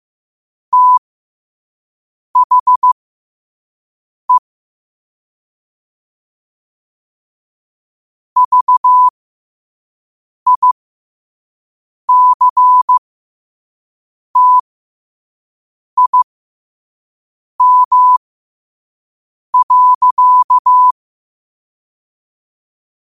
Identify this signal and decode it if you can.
Morse code, heard as a keyed tone, about 15 wpm, spaced as for 5 wpm, 1000 Hz. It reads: THE VICTIM.
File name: qotdfiles / 5wpm